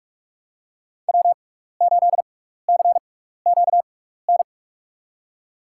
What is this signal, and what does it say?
Morse code, heard as a keyed tone, about 46 wpm, spaced as for 15 wpm, 700 Hz. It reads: W8CQN
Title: USA Call Signs - 5 Characters - 6x Character Spacing - 45wpm